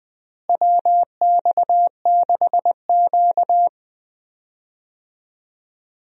Morse code, heard as a keyed tone, about 20 wpm, 700 Hz. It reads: WX6Q
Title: USA Call Signs - 4 Characters - Repeated 2x - 20wpm